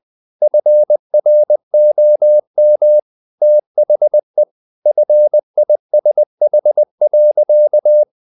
Morse code, heard as a keyed tone, about 20 wpm, 600 Hz. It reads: FROM THE FISH.
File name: the old man and the sea part 2 20wpm